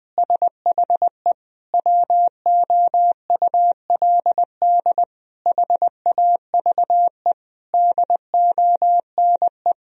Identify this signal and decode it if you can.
Morse code, heard as a keyed tone, about 20 wpm, 700 Hz. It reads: SHE WOULD HAVE DONE
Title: Sentences from Top 1000 Words - Review Part 3 - 20wpm